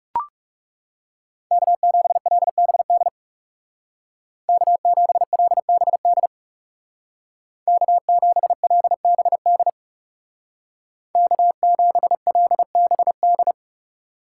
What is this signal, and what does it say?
Morse code, heard as a keyed tone, about 36 wpm, 700 Hz. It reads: K7LBD K7LBD K7LBD K7LBD
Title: Speed Racing - USA 1x3 Call Signs 30wpm